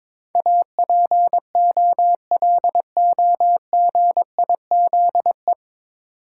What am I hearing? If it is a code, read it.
Morse code, heard as a keyed tone, about 22 wpm, 700 Hz. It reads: APOLOGIZE
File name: 9 Letter Words - First 3 Letters Given - 22wpm